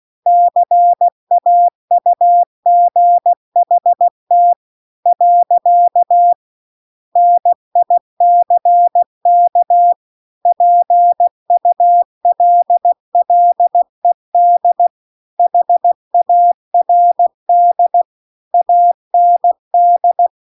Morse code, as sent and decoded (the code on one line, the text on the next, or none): -.-. .- ..- --. .... - / .-.-.- / -. .. -.-. -.- / .--. ..- .-.. .-.. -.. / .... .- .-. -.. / .- -. -..
CAUGHT . NICK PULLED HARD AND